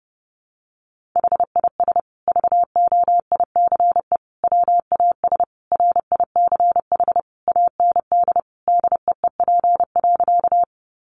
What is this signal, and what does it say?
Morse code, heard as a keyed tone, about 30 wpm, 700 Hz. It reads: HIS VOICE WAS RICH AND DEEP.
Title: Sentences from Top 1100 Words 30wpm